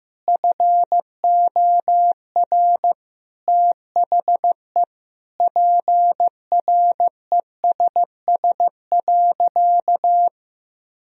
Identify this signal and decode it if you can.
Morse code, heard as a keyed tone, about 15 wpm, 700 Hz. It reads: FOR THE PRESS.
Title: Sentences from Top 1000 Words - Review Part 1 - 15wpm